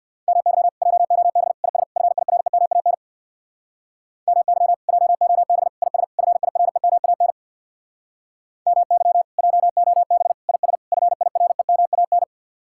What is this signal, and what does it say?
Morse code, heard as a keyed tone, about 58 wpm, 700 Hz. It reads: MY JOB IS FIREMAN MY JOB IS FIREMAN MY JOB IS FIREMAN
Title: Speed-Racing Common QSO Phrases Encore - 50wpm